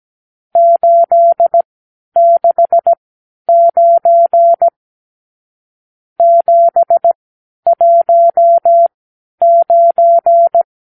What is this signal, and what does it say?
Morse code, heard as a keyed tone, about 17 wpm, spaced as for 10 wpm, 680 Hz. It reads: 869 719
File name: arrl / Review (1-0)